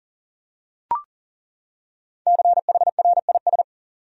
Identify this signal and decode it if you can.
Morse code, heard as a keyed tone, about 38 wpm, 700 Hz. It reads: CHRIS
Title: USA Common Names - 40wpm